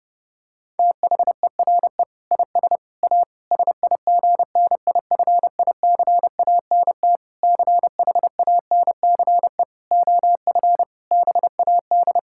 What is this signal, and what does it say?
Morse code, heard as a keyed tone, about 32 wpm, 700 Hz. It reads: THERE IS A SIGNIFICANT CHANCE OF BAD
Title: Sentences from Top 1000 Words - Review Part 6 - 30wpm